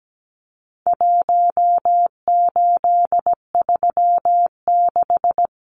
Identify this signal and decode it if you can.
Morse code, heard as a keyed tone, about 17 wpm, 700 Hz. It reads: 1836
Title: Numbers - Contest Style - 4-Digit 17wpm